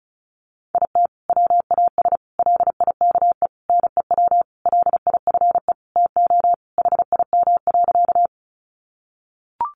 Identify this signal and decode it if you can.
Morse code, heard as a keyed tone, about 34 wpm, 700 Hz. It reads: IT WAS LIKE NEW LIFE TO HIM.
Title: Sentences from Top 300 Words 35wpm